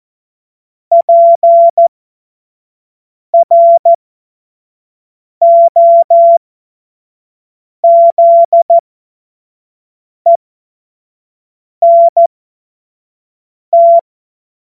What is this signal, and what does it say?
Morse code, heard as a keyed tone, about 14 wpm, spaced as for 5 wpm, 680 Hz. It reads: PROZENT
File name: Morsekurs / Witze / Witz 26. April 2022 25 F70